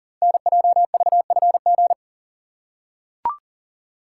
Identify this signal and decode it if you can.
Morse code, heard as a keyed tone, about 42 wpm, 700 Hz. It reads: NJVFG
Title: Code Groups - 5 Letters - 40wpm